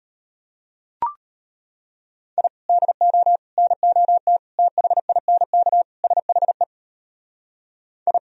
Morse code, heard as a keyed tone, about 32 wpm, 700 Hz. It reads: I DO NOT THINK SHE I